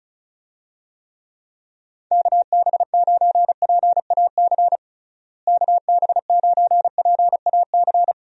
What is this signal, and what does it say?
Morse code, heard as a keyed tone, about 36 wpm, 700 Hz. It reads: KB9PAC KB9PAC
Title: USA Call Signs - 6 Characters - Repeated 2x - 35wpm